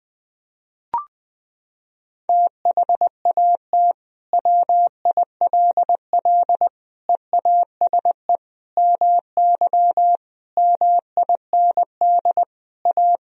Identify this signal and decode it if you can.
Morse code, heard as a keyed tone, about 20 wpm, 700 Hz. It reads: THAT WILL EASE MY MIND A
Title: Sentences from Top 1000 Words - Review Part 1 - 20wpm